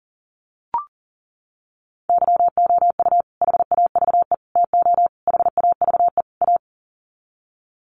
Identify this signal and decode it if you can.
Morse code, heard as a keyed tone, about 40 wpm, 700 Hz. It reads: YOU HAVE TO HAVE A
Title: Sets of 5 Words Encore 40wpm